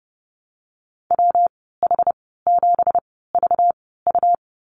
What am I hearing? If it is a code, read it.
Morse code, heard as a keyed tone, about 30 wpm, spaced as for 17 wpm, 700 Hz. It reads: WH7VU
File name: Lesson 37g - Letter x - Call Signs 30wpm